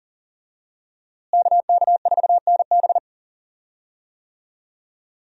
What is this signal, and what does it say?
Morse code, heard as a keyed tone, about 40 wpm, 690 Hz. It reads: KK4NB
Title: USA Call Signs - 5 Characters - Repeated 3x - 40wpm